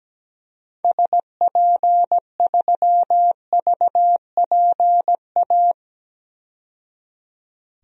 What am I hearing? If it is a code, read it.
Morse code, heard as a keyed tone, about 17 wpm, 700 Hz. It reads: SP3VPA
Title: DXCC-100 Call Signs - 6 Characters - 17wpm